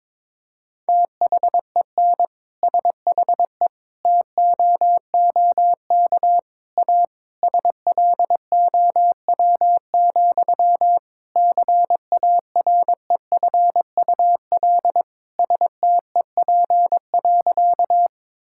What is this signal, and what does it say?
Morse code, heard as a keyed tone, about 22 wpm, 700 Hz. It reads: THEN SHE TOOK A SLOW, CAREFUL STEP.
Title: Sentences from Top 1600 Words 22wpm